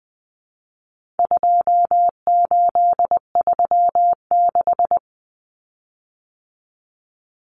Morse code, heard as a keyed tone, about 20 wpm, 700 Hz. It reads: 2836
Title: Common Cut Numbers - 4-Digit 20wpm